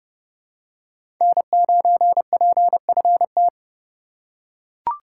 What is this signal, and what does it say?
Morse code, heard as a keyed tone, about 32 wpm, 700 Hz. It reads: N9PFT
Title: USA Call Signs - 5 Characters - 2x Character Spacing - 30wpm